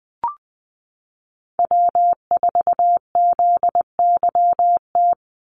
Morse code, heard as a keyed tone, about 20 wpm, 700 Hz. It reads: W4ZYT
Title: Call Signs - 20wpm